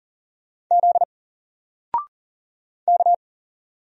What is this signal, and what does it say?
Morse code, heard as a keyed tone, about 39 wpm, 700 Hz. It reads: Z K
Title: Single Character (Letter:Number:Punctuation:Prosign) Alt 40wpm